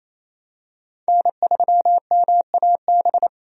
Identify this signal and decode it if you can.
Morse code, heard as a keyed tone, about 28 wpm, 700 Hz. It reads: N3MAB
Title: USA Call Signs - 5 Characters - 28wpm